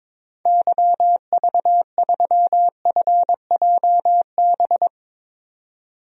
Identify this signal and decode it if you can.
Morse code, heard as a keyed tone, about 22 wpm, 700 Hz. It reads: YV3FJB
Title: DXCC-100 Call Signs - 6 Characters - 22wpm